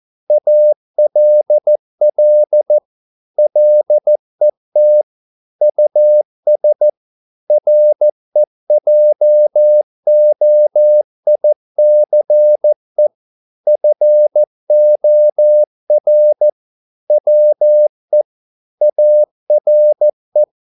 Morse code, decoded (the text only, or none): ALL LET US REJOICE FOR WE ARE